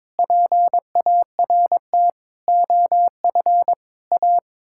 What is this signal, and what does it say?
Morse code, heard as a keyed tone, about 22 wpm, 700 Hz. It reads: PART OF A
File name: practice / 22wpm / Sets of 6 Words - 22wpm